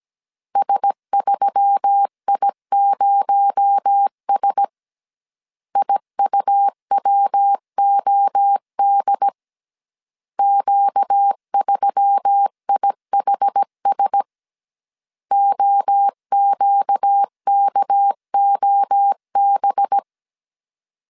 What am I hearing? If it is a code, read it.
Morse code, heard as a keyed tone, about 17 wpm, 800 Hz. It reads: S3I0S IUWOD Q3IHS OQKOB
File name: CW / DET / 1 6 1500C S A 16wpm 2635